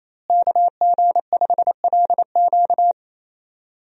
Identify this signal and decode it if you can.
Morse code, heard as a keyed tone, about 28 wpm, 700 Hz. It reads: KG5LQ
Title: USA Call Signs - 5 Characters - Repeated 3x - 28wpm